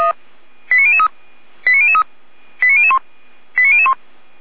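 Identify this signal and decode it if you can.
Five-tone selective call (ZVEI), heard as quick runs of five tones, tones of about 70 ms. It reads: 89003 89003 89001 89001
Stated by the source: ZVEI five-tone call received on an SDR.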